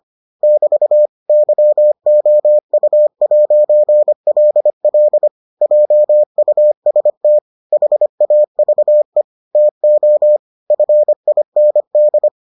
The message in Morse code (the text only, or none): <BT> YOU'LL JUST HAVE TO FIND